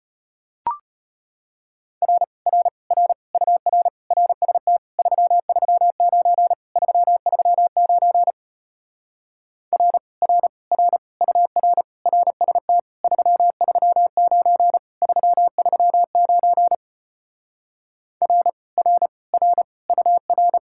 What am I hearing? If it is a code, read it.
Morse code, heard as a keyed tone, about 33 wpm, 700 Hz. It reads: R R R UR RST 339 339 R R R UR RST 339 339 R R R UR